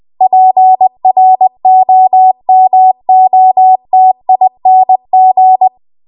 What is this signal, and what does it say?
Morse code, heard as a keyed tone, about 20 wpm, 750 Hz. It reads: PROMOTING
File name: Archive / 20 WPM / 20WPM